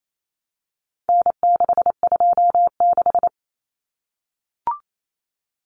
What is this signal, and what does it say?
Morse code, heard as a keyed tone, about 28 wpm, 700 Hz. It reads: N626
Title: Common Cut Numbers - 4-Digit 28wpm